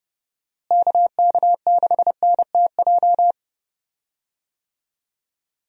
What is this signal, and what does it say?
Morse code, heard as a keyed tone, about 31 wpm, 700 Hz. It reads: KK6NTJ
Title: USA Call Signs - 6 Characters - Repeated 3x - 30wpm